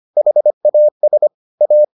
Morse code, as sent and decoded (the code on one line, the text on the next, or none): .... .- ... / .-
HAS A